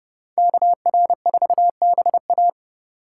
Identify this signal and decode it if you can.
Morse code, heard as a keyed tone, about 29 wpm, 700 Hz. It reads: KR4BA